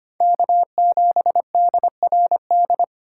Morse code, heard as a keyed tone, about 25 wpm, 700 Hz. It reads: K7DRD
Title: USA Call Signs - 4x Character Spacing - 25wpm